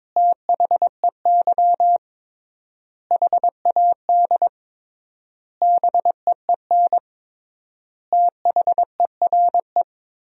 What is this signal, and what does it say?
Morse code, heard as a keyed tone, about 22 wpm, 700 Hz. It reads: THEY HAD BEEN THERE
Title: 3x Word-Spacing - Sentences from Top 400 Words - 22wpm